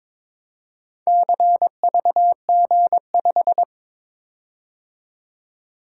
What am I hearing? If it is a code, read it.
Morse code, heard as a keyed tone, about 22 wpm, 700 Hz. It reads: CVG5
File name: practice / 22wpm / Four Letter-Number Combinations - 22wpm